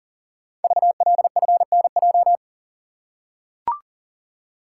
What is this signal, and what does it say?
Morse code, heard as a keyed tone, about 38 wpm, 700 Hz. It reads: VLFNJ
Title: Code Groups - 5 Letters - 40wpm